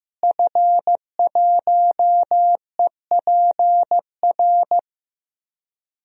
Morse code, heard as a keyed tone, about 15 wpm, 700 Hz. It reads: F1EPR
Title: DXCC-100 Call Signs - 5 Characters - Repeated 3x - 15wpm